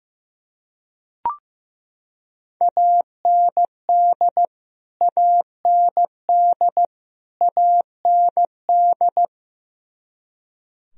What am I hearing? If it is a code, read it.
Morse code, heard as a keyed tone, about 15 wpm, 700 Hz. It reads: AND AND AND
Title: Top 100 Words Repeated 15wpm